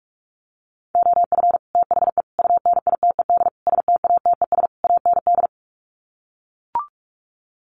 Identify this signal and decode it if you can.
Morse code, heard as a keyed tone, about 44 wpm, 700 Hz. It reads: OF THE UNITED STATES AND
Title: Sets of 5 Words 45wpm